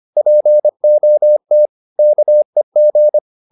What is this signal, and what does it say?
Morse code, heard as a keyed tone, about 25 wpm, 600 Hz. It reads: POT KEG
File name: PR204a 25